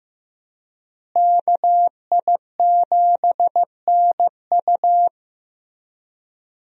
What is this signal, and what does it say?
Morse code, heard as a keyed tone, about 15 wpm, 700 Hz. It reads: KI7NU
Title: USA Call Signs - 5 Characters - Repeated 3x - 15wpm